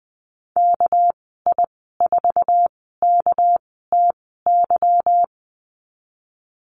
Morse code, heard as a keyed tone, about 19 wpm, spaced as for 14 wpm, 700 Hz. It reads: KI4KTY
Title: 245 Lesson 39g - Letter z - Call Signs 20wpm